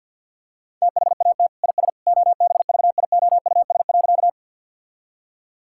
Morse code, heard as a keyed tone, about 51 wpm, 700 Hz. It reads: THAT IS OBVIOUS.
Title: Sentences from Top 1600 Words 50wpm